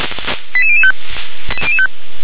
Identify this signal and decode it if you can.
Five-tone selective call (ZVEI), heard as quick runs of five tones, tones of about 70 ms. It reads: 88005